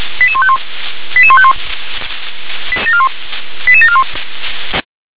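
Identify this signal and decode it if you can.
Five-tone selective call (ZVEI), heard as quick runs of five tones, tones of about 80 ms. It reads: 88151 88151 88741
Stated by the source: ZVEI five-tone call received by SDR.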